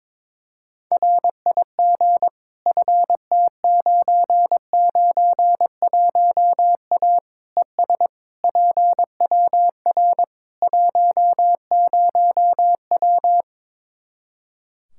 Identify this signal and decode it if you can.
Morse code, heard as a keyed tone, about 22 wpm, 700 Hz. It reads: RIG FT991A ES PWR 10W
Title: Practice QSO - Exchange 2 - Info - 22wpm